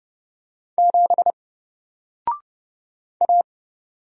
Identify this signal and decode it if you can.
Morse code, heard as a keyed tone, about 30 wpm, 700 Hz. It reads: 7 A
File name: Single-Letter-Number-30wpm